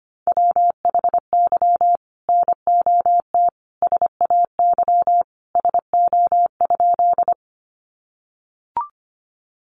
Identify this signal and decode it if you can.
Morse code, heard as a keyed tone, about 25 wpm, 700 Hz. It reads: WHY NOT SAY SO?
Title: Sentences from Top 600 Words 25wpm